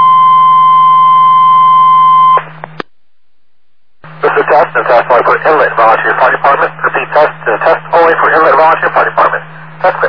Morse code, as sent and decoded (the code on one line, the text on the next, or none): none